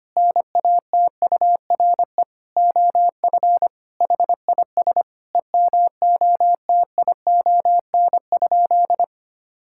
Morse code, heard as a keyed tone, about 25 wpm, 700 Hz. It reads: NATURE OF HIS EMOTION?
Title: Sentences from Top 1600 Words 25wpm